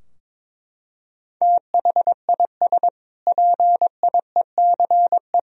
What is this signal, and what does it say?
Morse code, heard as a keyed tone, about 22 wpm, 700 Hz. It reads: THIS PIECE
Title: Sentences from Top 1000 Words - Review Part 2 - 22wpm